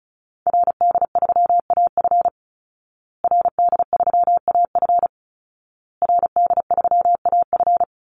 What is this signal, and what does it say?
Morse code, heard as a keyed tone, about 34 wpm, 700 Hz. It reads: RD3AF RD3AF RD3AF
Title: Call Signs Repeated 2x - 35wpm